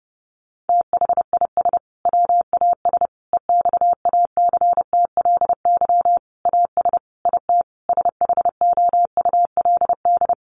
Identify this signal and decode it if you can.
Morse code, heard as a keyed tone, about 30 wpm, 700 Hz. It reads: THIS WAS EXACTLY AS IT SHOULD
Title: Sentences from Top 900 Words 30wpm